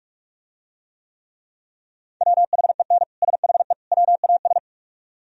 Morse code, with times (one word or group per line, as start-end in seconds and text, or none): 2.2-3.0 WHEN
3.2-3.7 SHE
3.9-4.6 WAS